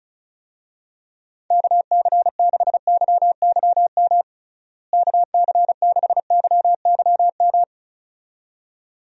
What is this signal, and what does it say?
Morse code, heard as a keyed tone, about 34 wpm, 700 Hz. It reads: KC6YYM KC6YYM
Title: USA Call Signs - 6 Characters - Repeated 2x - 35wpm